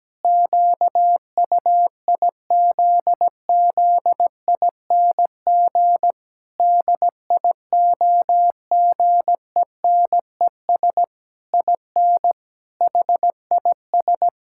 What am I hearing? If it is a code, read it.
Morse code, heard as a keyed tone, about 17 wpm, 700 Hz. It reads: QUIZZING DIOGENES IN HIS
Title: Pangram Challenge v2 - 17wpm